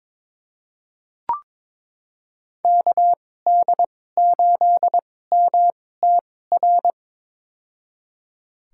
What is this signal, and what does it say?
Morse code, heard as a keyed tone, about 22 wpm, spaced as for 16 wpm, 700 Hz. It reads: KD8MTR
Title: USA Call Signs - 6 Characters - 2x Character Spacing - 22wpm